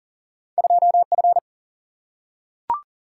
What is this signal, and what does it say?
Morse code, heard as a keyed tone, about 38 wpm, 700 Hz. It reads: 2F
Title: Single-Letter Single-Number 40wpm